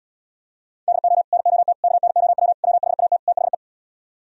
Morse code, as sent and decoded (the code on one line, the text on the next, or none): -.. --- / -. --- - / -.- -. --- .-- / .-- .... .- - / - .... .
DO NOT KNOW WHAT THE